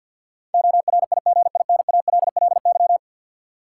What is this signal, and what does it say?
Morse code, heard as a keyed tone, about 53 wpm, 700 Hz. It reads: ORIGINALLY
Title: Sentences from Top 1500 Words - New Word Drill - 50wpm